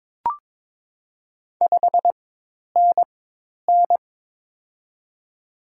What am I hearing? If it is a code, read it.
Morse code, heard as a keyed tone, about 22 wpm, spaced as for 10 wpm, 700 Hz. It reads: 5NN
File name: Top 20 QSO Elements - 4x Character Spacing - 22wpm